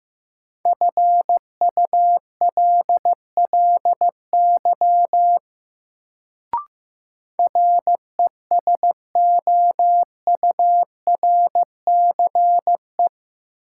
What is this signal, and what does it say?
Morse code, heard as a keyed tone, about 15 wpm, 700 Hz. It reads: FULLY RESOURCE
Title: Sentences from Top 1300 Words - New Word Drill - 15wpm